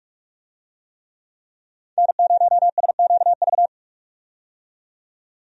Morse code, heard as a keyed tone, about 46 wpm, 700 Hz. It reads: N0SQV
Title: USA Call Signs - 5 Characters - 45wpm